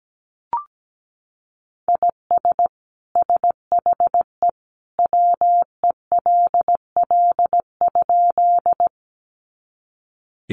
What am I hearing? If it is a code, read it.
Morse code, heard as a keyed tone, about 17 wpm, 700 Hz. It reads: IS SHE WELL?
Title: Sentences from Top 200 Words 17wpm